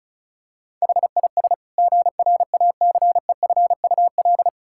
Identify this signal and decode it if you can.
Morse code, heard as a keyed tone, about 36 wpm, 700 Hz. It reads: HIS GRACEFUL
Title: Pangram Challenge v2 - 35wpm